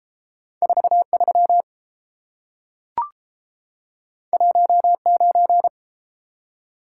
Morse code, heard as a keyed tone, about 30 wpm, 700 Hz. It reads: 43 19